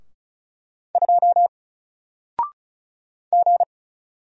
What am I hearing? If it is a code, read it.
Morse code, heard as a keyed tone, about 35 wpm, 700 Hz. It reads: 2 G
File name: Single-Letter-Number-35wpm